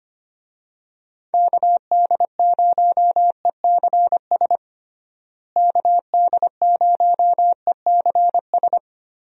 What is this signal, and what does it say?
Morse code, heard as a keyed tone, about 25 wpm, 700 Hz. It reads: KD0ECS KD0ECS
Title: USA Call Signs - 6 Characters - Repeated 3x - 25wpm